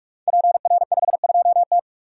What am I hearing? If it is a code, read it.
Morse code, heard as a keyed tone, about 44 wpm, 700 Hz. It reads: PR52T
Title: Code Groups - 5 Letter:Numbers - 45wpm